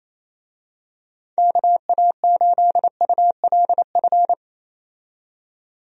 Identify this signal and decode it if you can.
Morse code, heard as a keyed tone, about 27 wpm, 700 Hz. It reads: KA8ULF
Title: USA Call Signs - 6 Characters - 28wpm